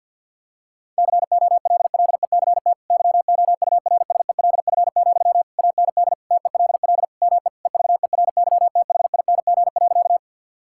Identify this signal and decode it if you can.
Morse code, heard as a keyed tone, about 50 wpm, 700 Hz. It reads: COLLECT YOURSELF, AND TELL ME EVERYTHING.
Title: Sentences from Top 1600 Words 50wpm